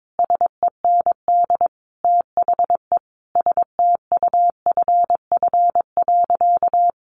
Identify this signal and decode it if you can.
Morse code, heard as a keyed tone, about 22 wpm, 700 Hz. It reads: SEND THE STUFF.
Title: Sentences from Top 1500 Words 22wpm